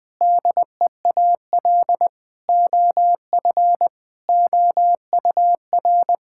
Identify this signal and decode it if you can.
Morse code, heard as a keyed tone, about 21 wpm, 700 Hz. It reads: DEAL OF OUR
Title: Sentences from Top 1000 Words - Review Part 6 - 20wpm